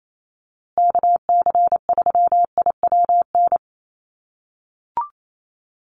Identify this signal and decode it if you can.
Morse code, heard as a keyed tone, about 28 wpm, 700 Hz. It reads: KC3IWN
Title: USA Call Signs Repeated 2x 28wpm